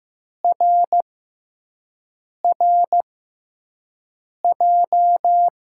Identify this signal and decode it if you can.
Morse code, heard as a keyed tone, about 15 wpm, spaced as for 5 wpm, 700 Hz. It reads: RRJ
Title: USA Call Signs - 5 Characters - 6x Character Spacing - 15wpm